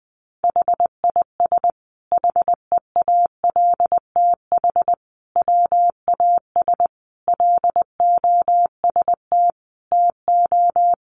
Morse code, heard as a keyed tone, about 20 wpm, 700 Hz. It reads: HIS HEALTH WAS LOST TO
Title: Sentences from Top 1400 Words 20wpm